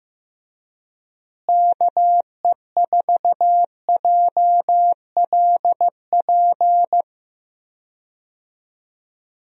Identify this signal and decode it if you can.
Morse code, heard as a keyed tone, about 15 wpm, 700 Hz. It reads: KE4JLP
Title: USA Call Signs - 6 Characters - Repeated 3x - 15wpm